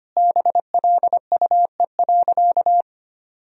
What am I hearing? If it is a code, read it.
Morse code, heard as a keyed tone, about 25 wpm, 700 Hz. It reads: BLUE.
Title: Sentences from Top 1000 Words - Review Part 3 - 25wpm